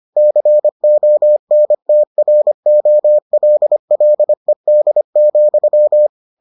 Morse code, heard as a keyed tone, about 25 wpm, 600 Hz. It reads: CONTROLLED,